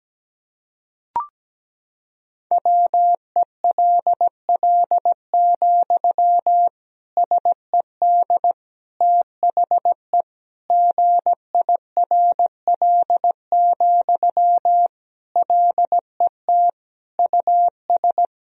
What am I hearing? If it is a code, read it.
Morse code, heard as a keyed tone, about 17 wpm, 700 Hz. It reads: WELL, SED THE GIRL, LET US
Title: Sentences from Top 400 Words and Abbreviations 17wpm